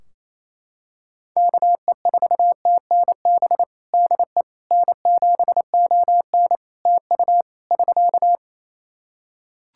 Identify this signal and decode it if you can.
Morse code, heard as a keyed tone, about 27 wpm, 700 Hz. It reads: KE4TNB DE N7ON TU <SK>
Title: Practice QSO - Exchange 4 - Ending - 28wpm